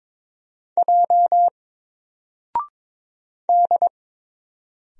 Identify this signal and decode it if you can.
Morse code, heard as a keyed tone, about 22 wpm, 700 Hz. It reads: J D
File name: Single-Letter-Number-22wpm